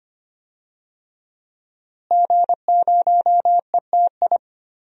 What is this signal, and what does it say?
Morse code, heard as a keyed tone, about 25 wpm, 700 Hz. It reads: G0ETI
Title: DXCC-100 Call Signs - 5 Characters - Repeated 3x - 25wpm